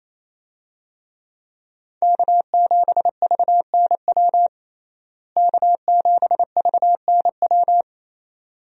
Morse code, heard as a keyed tone, about 28 wpm, 700 Hz. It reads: K7VNW K7VNW
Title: USA Call Signs - 5 Characters - Repeated 2x - 28wpm